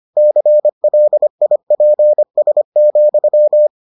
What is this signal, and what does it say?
Morse code, heard as a keyed tone, about 25 wpm, 600 Hz. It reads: CLIPS,